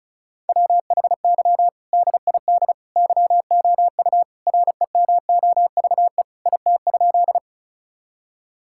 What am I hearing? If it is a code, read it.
Morse code, heard as a keyed tone, about 34 wpm, 700 Hz. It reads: WHY DID YOU REMOVE IT?
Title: Sentences from Top 1000 Words - Review Part 5 - 35wpm